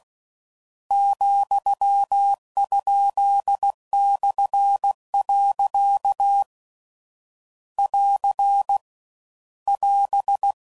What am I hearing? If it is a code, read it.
Morse code, heard as a keyed tone, about 16 wpm, 800 Hz. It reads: ,?/. <AR> <AS>